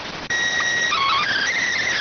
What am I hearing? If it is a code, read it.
SSTV (slow-scan television), PD 290.